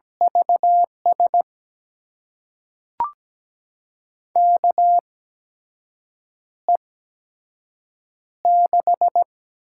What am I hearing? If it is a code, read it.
Morse code, heard as a keyed tone, about 17 wpm, spaced as for 6 wpm, 700 Hz. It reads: VS KE6
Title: USA Call Signs - 6 Characters - 8x Character Spacing - 17wpm